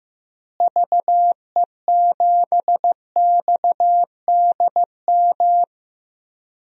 Morse code, heard as a keyed tone, about 16 wpm, 700 Hz. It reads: VE7XDM
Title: DXCC-100 Call Signs - 6 Characters - Repeated 3x - 15wpm